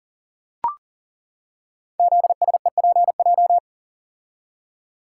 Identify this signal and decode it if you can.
Morse code, heard as a keyed tone, about 40 wpm, 690 Hz. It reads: ZSEPJ
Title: Code Groups - 5 Letters - 40wpm